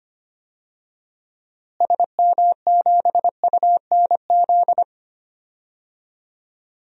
Morse code, heard as a keyed tone, about 25 wpm, 700 Hz. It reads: SM7UNZ
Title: DXCC-100 Call Signs - 6 Characters - Repeated 2x - 25wpm